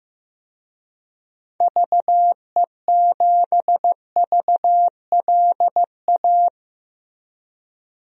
Morse code, heard as a keyed tone, about 15 wpm, 700 Hz. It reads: VE7VLA
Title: DXCC-100 Call Signs - 6 Characters - Repeated 2x - 15wpm